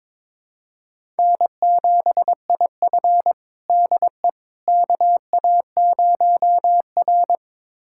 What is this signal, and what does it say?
Morse code, heard as a keyed tone, about 22 wpm, 700 Hz. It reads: N7IF DE KA0R
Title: USA Callsign DE Callsign - 4 Characters - 22wpm